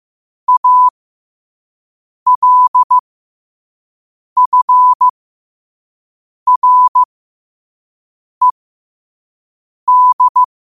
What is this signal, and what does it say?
Morse code, heard as a keyed tone, about 15 wpm, spaced as for 5 wpm, 1000 Hz. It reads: ALFRED
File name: qotdfiles / 5wpm